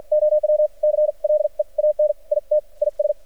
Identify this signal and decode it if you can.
Morse code, heard as a keyed tone, about 56 wpm, spaced as for 43 wpm, 600 Hz. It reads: OW CLEAN IT IS
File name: archived / CRSnet